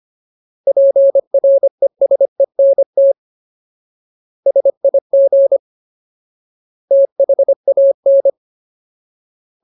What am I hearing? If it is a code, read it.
Morse code, heard as a keyed tone, about 25 wpm, 550 Hz. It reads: PRESENT SIG THAN
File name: Ultimate Word List / 25WPM Ultimate